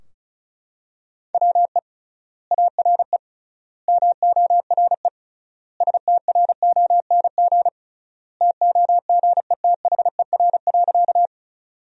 Morse code, heard as a keyed tone, about 34 wpm, 700 Hz. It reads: WE ARE MORE STRONG TOGETHER.
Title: 3x Word-Spacing - Sentences from Top 500 Words - 35wpm